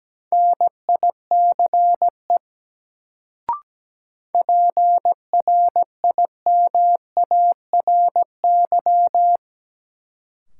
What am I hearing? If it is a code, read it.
Morse code, heard as a keyed tone, about 17 wpm, 700 Hz. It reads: NICE PRIMARY
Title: Sentences from Top 1100 Words - New Word Drill Alt - 17wpm